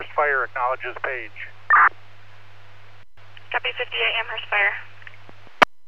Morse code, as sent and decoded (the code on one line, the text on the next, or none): none